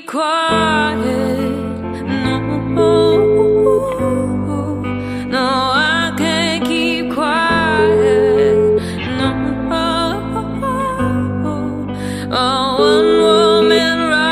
Ringback tone (Europe, 425 Hz), with noise.